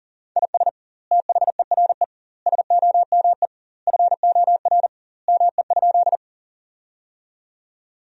Morse code, heard as a keyed tone, about 40 wpm, 700 Hz. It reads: IS THERE SOME FOR ME?
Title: 2x Word-Spacing - Sentences from Top 200 Words - 40wpm